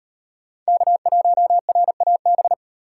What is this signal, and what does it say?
Morse code, heard as a keyed tone, about 38 wpm, 700 Hz. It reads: K1RAB